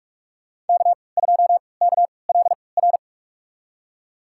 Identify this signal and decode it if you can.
Morse code, heard as a keyed tone, about 46 wpm, spaced as for 25 wpm, 700 Hz. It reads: K2KLR
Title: USA Call Signs - 5 Characters - 3x Character Spacing - 45wpm